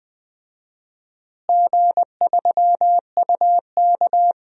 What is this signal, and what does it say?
Morse code, heard as a keyed tone, about 20 wpm, 700 Hz. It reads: G3UK
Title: DXCC-100 Call Signs - 4 Characters - Repeated 3x - 20wpm